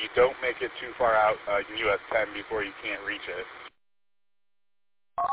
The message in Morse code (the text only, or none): none